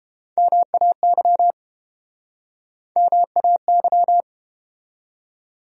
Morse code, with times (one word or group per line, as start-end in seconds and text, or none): 0.4-1.5 MAY
3.0-4.2 MAY